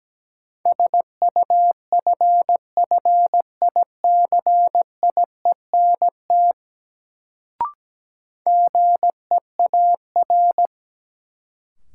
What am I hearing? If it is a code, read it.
Morse code, heard as a keyed tone, about 17 wpm, 700 Hz. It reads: SUFFICIENT GEAR
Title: Sentences from Top 1400 Words - New Word Drill Alt - 17wpm